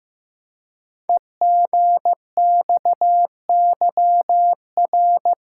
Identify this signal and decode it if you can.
Morse code, heard as a keyed tone, about 15 wpm, 700 Hz. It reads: EGXYR
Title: Code Groups - 5 Letters - 15wpm